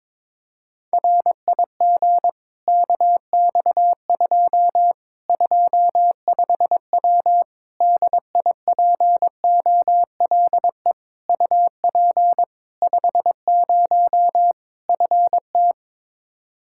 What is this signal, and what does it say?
Morse code, heard as a keyed tone, about 22 wpm, 700 Hz. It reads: RIG KX2 25W DIPOLE UP 50 FT
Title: K1USN SST QSO Practice 22wpm